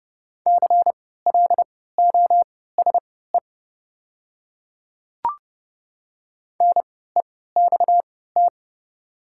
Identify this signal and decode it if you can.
Morse code, heard as a keyed tone, about 31 wpm, spaced as for 17 wpm, 700 Hz. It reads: CLOSE NEXT
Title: 3x Character Spacing - Top 300 Words - 30wpm